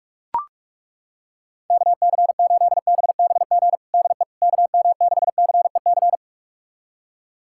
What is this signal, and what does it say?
Morse code, heard as a keyed tone, about 41 wpm, 700 Hz. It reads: KC8BBG DE KM6CEC
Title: USA Callsign DE Callsign 45wpm